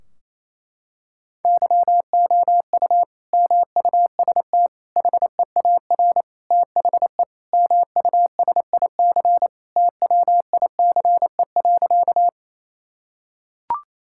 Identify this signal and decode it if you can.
Morse code, heard as a keyed tone, about 28 wpm, 700 Hz. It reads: YOU MUST HEAR THE MUSIC TWICE.
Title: Sentences from Top 1600 Words 28wpm